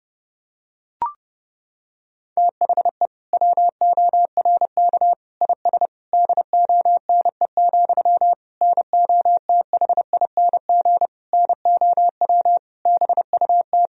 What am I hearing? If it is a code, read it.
Morse code, heard as a keyed tone, about 32 wpm, 700 Hz. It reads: THE WORK IS DONE, NOTHING NOW BUT
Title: Sentences from Top 1000 Words - Review Part 4 - 30wpm